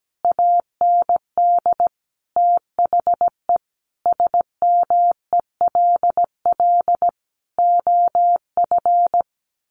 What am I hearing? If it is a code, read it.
Morse code, heard as a keyed tone, about 17 wpm, 700 Hz. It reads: AND THE SMELL OF
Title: A Sound Of Thunder 17wpm